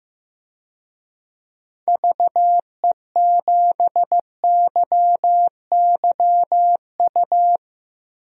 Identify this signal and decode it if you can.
Morse code, heard as a keyed tone, about 15 wpm, 700 Hz. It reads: VE7YYU
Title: DXCC-100 Call Signs - 6 Characters - Repeated 2x - 15wpm